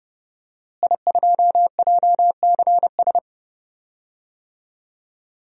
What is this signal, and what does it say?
Morse code, heard as a keyed tone, about 30 wpm, 700 Hz. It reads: I2JCS
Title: DXCC-100 Call Signs - 5 Characters - Repeated 2x - 30wpm